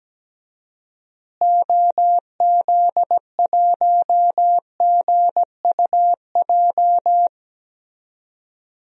Morse code, heard as a keyed tone, about 17 wpm, 700 Hz. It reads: OZ1GUJ
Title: DXCC-100 Call Signs - 6 Characters - Repeated 3x - 17wpm